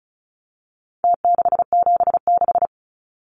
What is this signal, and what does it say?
Morse code, heard as a keyed tone, about 34 wpm, 700 Hz. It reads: T676
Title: Numbers - Contest Style - 4-Digit 35wpm